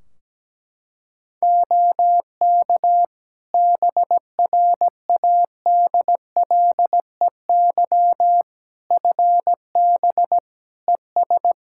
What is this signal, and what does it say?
Morse code, heard as a keyed tone, about 17 wpm, 700 Hz. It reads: OK BRADLEY FB ES
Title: Practice QSO - Exchange 2 - Info - 17wpm